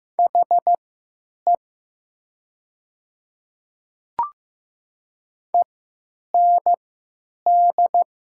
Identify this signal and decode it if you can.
Morse code, heard as a keyed tone, about 15 wpm, spaced as for 9 wpm, 700 Hz. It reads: HE END
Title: 3x Character Spacing - Top 500 Words - 15wpm